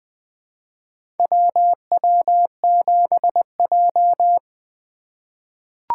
Morse code, heard as a keyed tone, about 20 wpm, 700 Hz. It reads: WW7J
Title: USA Call Signs - 4 Characters - 2x Character Spacing - 20wpm